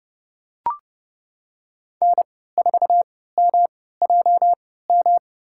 Morse code, heard as a keyed tone, about 30 wpm, spaced as for 17 wpm, 700 Hz. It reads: N4MJM
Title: USA Call Signs - 5 Characters - 3x Character Spacing - 30wpm